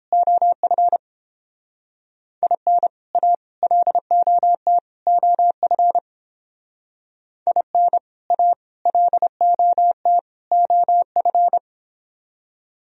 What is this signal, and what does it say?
Morse code, heard as a keyed tone, about 29 wpm, 700 Hz. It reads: OF IN A LOT OF IN A LOT OF